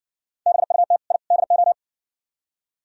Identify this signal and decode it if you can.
Morse code, heard as a keyed tone, about 58 wpm, 700 Hz. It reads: BUT I DO